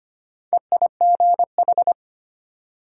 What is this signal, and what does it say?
Morse code, heard as a keyed tone, about 34 wpm, spaced as for 30 wpm, 700 Hz. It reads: EIGH
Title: Speed-Racing Multi-letter Phonemes - 25wpm